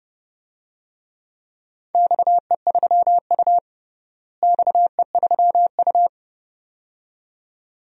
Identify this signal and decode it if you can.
Morse code, heard as a keyed tone, about 30 wpm, 700 Hz. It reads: XE3U XE3U
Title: DXCC-100 Call Signs - 4 Characters - Repeated 2x - 30wpm